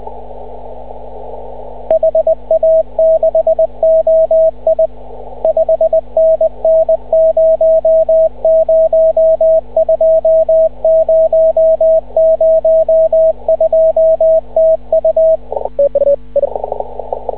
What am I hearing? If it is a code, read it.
Morse code, heard as a keyed tone, about 20 wpm, 640 Hz, with radio noise. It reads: HA6OI 5NN 002 002 TU
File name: CW / ha6oi 2